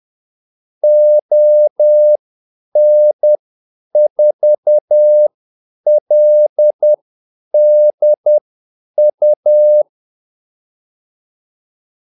Morse code, heard as a keyed tone, about 10 wpm, spaced as for 8 wpm, 600 Hz. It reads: ON4LDU
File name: Callsigns-10-8wpm